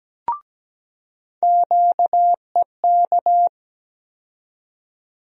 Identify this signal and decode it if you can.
Morse code, heard as a keyed tone, about 17 wpm, 700 Hz. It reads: QEK
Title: USA Call Sign Postfixes - 3 Letters 17wpm